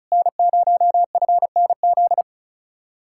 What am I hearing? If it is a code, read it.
Morse code, heard as a keyed tone, about 44 wpm, 700 Hz. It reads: N0FNZ